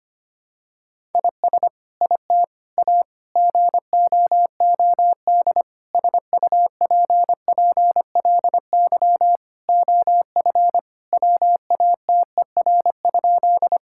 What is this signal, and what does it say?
Morse code, heard as a keyed tone, about 25 wpm, 700 Hz. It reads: IS IT A GOOD SUPPLY OF WATER?
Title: Sentences from Top 1000 Words - Review Part 5 - 25wpm